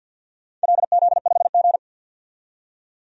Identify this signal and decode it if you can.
Morse code, heard as a keyed tone, about 51 wpm, 700 Hz. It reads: LZ5G
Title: DXCC-100 Call Signs - 4 Characters - 50wpm